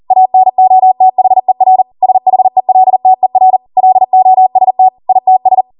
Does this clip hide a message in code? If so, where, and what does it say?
Yes: Morse code from 0.1 to 5.6 s, heard as a keyed tone, about 40 wpm, 750 Hz. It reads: ANOTHER SHELTER LOST ITS